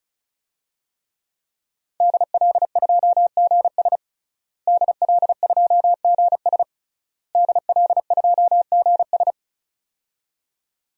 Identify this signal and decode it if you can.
Morse code, heard as a keyed tone, about 36 wpm, 700 Hz. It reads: DL2GS DL2GS DL2GS
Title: DXCC-100 Call Signs - 5 Characters - Repeated 3x - 35wpm